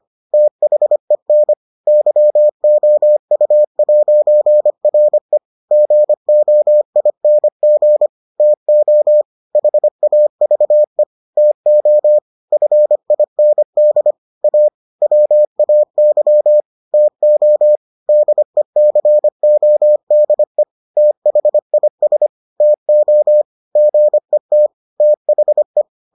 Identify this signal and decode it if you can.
Morse code, heard as a keyed tone, about 25 wpm, 600 Hz. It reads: THEN YOU'RE GOING TO HAVE TO FIND A WAY TO DECODE THIS TO GET THE